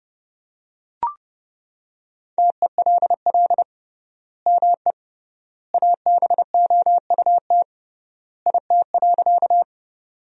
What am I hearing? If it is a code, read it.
Morse code, heard as a keyed tone, about 32 wpm, 700 Hz. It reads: TELL ME ABOUT IT.
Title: 3x Word-Spacing - Sentences from Top 200 Words - 30wpm